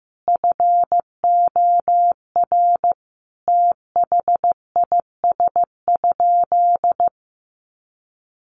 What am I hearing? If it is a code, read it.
Morse code, heard as a keyed tone, about 14 wpm, 700 Hz. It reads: FOR THIS?
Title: Sentences from Top 1400 Words 15wpm